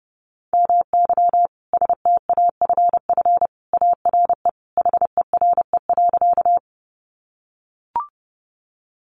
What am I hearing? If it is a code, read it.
Morse code, heard as a keyed tone, about 30 wpm, 700 Hz. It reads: MY STAFF ARE HERE.
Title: Sentences from Top 1200 Words 30wpm